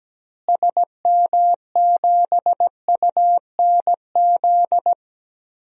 Morse code, heard as a keyed tone, about 17 wpm, 700 Hz. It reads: SM7UNZ
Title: DXCC-100 Call Signs - 6 Characters - Repeated 2x - 17wpm